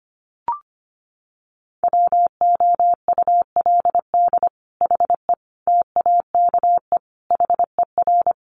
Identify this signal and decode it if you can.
Morse code, heard as a keyed tone, about 25 wpm, 700 Hz. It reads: WOULD HE TAKE HER
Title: Sentences from Top 1500 Words 25wpm